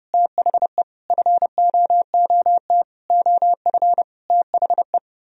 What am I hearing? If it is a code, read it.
Morse code, heard as a keyed tone, about 29 wpm, 700 Hz. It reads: THE FOOT OF THE